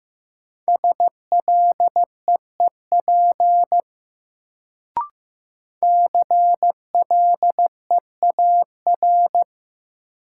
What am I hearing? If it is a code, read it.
Morse code, heard as a keyed tone, about 15 wpm, 700 Hz. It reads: SLEEP CLEAR
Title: Sentences from Top 500 Words - New Word Drill - 15wpm